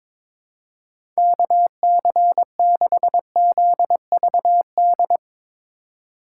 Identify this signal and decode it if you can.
Morse code, heard as a keyed tone, about 22 wpm, 700 Hz. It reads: KC6ZVD
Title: USA Call Signs - 6 Characters - 2x Character Spacing - 22wpm